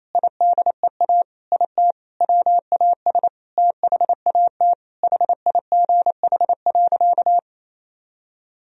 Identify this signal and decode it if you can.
Morse code, heard as a keyed tone, about 27 wpm, 700 Hz. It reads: IDEA IT WAS THAT HIGH.
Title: Sentences from Top 1000 Words - Review Part 1 - 28wpm